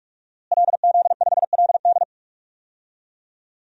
Morse code, heard as a keyed tone, about 41 wpm, 700 Hz. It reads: LZ5LD
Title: DXCC-100 Call Signs - 5 Characters - Repeated 2x - 45wpm